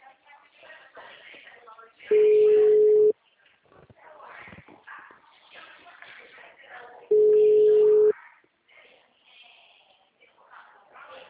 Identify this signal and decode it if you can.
Ringback tone (Europe, 425 Hz), with speech.
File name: out- -2003- - - .473